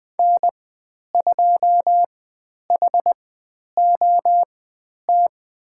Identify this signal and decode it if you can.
Morse code, heard as a keyed tone, about 20 wpm, spaced as for 10 wpm, 700 Hz. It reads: N2HOT
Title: 102 Lesson 17g - Letter u - Call Signs 20wpm